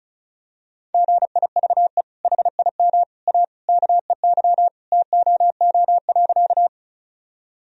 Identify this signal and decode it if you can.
Morse code, heard as a keyed tone, about 34 wpm, 700 Hz. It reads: GIVE HIM A KEY TOO.
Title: Sentences from Top 1000 Words - Review Part 4 - 35wpm